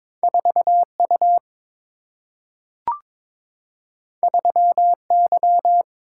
Morse code, heard as a keyed tone, about 22 wpm, 700 Hz. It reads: 4 U 3 Y
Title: Single-Letter Single-Number 22wpm